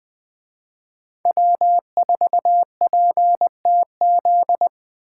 Morse code, heard as a keyed tone, about 20 wpm, 700 Hz. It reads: W4PTZ
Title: Speed Racing - USA Call Signs 17wpm